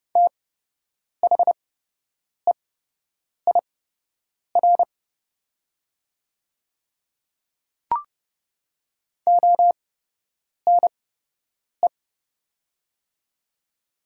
Morse code, heard as a keyed tone, about 29 wpm, spaced as for 8 wpm, 700 Hz. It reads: THEIR ONE
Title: 8x Character Spacing - Top 100 Words - 30wpm